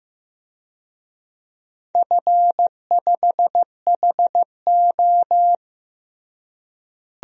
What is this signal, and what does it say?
Morse code, heard as a keyed tone, about 15 wpm, 700 Hz. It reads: F5HO
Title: DXCC-100 Call Signs - 4 Characters - Repeated 3x - 15wpm